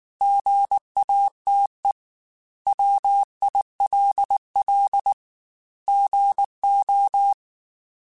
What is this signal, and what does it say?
Morse code, heard as a keyed tone, about 19 wpm, 800 Hz. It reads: GATE WILL GO
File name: Technical Subjects / Technical Subjects-mp3s / OR Gate @18